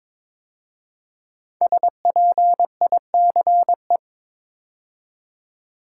Morse code, heard as a keyed tone, about 22 wpm, 700 Hz. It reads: SPICE
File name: 5 Letter Words - First 3 Letters Given - 22wpm